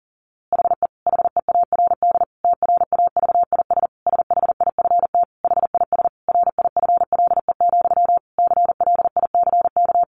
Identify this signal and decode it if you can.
Morse code, heard as a keyed tone, about 40 wpm, 700 Hz. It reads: HE HEARD TRAVIS SHIFT HIS RIFLE, CLICK
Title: A Sound Of Thunder 40wpm